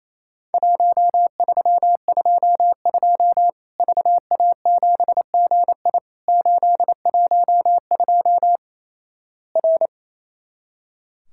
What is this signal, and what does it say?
Morse code, heard as a keyed tone, about 28 wpm, 700 Hz. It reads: 1322 VA7GI 812 R
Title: Worked All Europe DX Contest - QTC Exchange - 28wpm